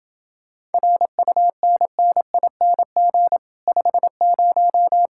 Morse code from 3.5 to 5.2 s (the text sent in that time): 50